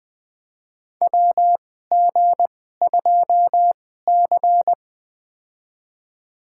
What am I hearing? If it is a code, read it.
Morse code, heard as a keyed tone, about 20 wpm, spaced as for 15 wpm, 700 Hz. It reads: WG2C
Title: USA Call Signs - 4 Characters - 2x Character Spacing - 20wpm